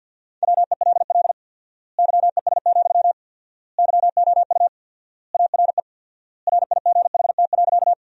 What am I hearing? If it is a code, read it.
Morse code, heard as a keyed tone, about 53 wpm, 700 Hz. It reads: WELL YES, YOU ARE RIGHT.
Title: 4x Word-Spacing - Sentences from Top 500 Words - 50wpm